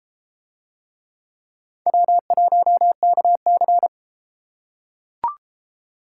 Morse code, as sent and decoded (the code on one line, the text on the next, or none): .-- .---- -.- -.-.
W1KC